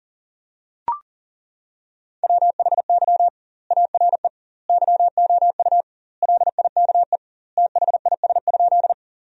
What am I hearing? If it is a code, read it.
Morse code, heard as a keyed tone, about 40 wpm, 700 Hz. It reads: WHY ARE YOU LIKE THIS?
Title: 2x Word-Spacing - Sentences from Top 200 Words - 40wpm